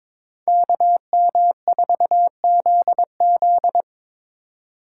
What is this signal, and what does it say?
Morse code, heard as a keyed tone, about 22 wpm, 700 Hz. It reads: KM4ZZ
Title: K1USN SST QSO Practice - 22wpm